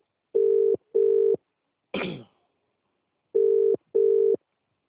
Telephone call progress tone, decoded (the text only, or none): ringback tone